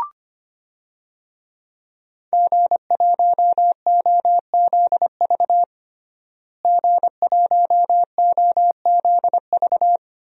Morse code, heard as a keyed tone, about 25 wpm, 700 Hz. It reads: G1OZV G1OZV
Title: DXCC-100 Call Signs - 5 Characters - Repeated 2x - 25wpm